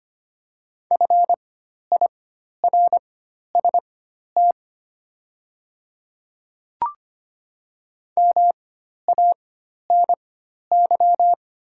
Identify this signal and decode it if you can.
Morse code, heard as a keyed tone, about 25 wpm, spaced as for 12 wpm, 690 Hz. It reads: FIRST MANY
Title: 4x Character Spacing - Top 200 Words - 25wpm